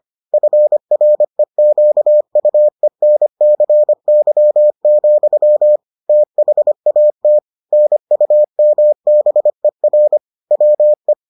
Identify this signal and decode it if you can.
Morse code, heard as a keyed tone, about 25 wpm, 600 Hz. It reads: FREQUENCY, THAT NUMBER WE